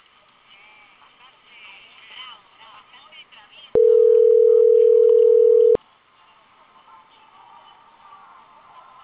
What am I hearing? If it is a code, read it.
Ringback tone (North America), 440 + 480 Hz, with speech.
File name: CENTRAL CAEA / out agcaea02